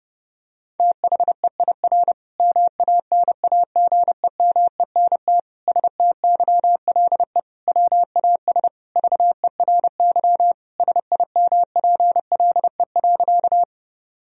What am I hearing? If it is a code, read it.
Morse code, heard as a keyed tone, about 30 wpm, 700 Hz. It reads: THEIR MANAGEMENT STYLE WAS VERY SIMPLE.
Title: Sentences from Top 1000 Words - Review Part 4 - 30wpm